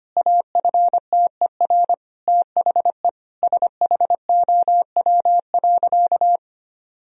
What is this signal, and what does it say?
Morse code, heard as a keyed tone, about 25 wpm, 700 Hz. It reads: AFTER THE SHOW.
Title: Sentences from Top 1000 Words - Review Part 3 - 25wpm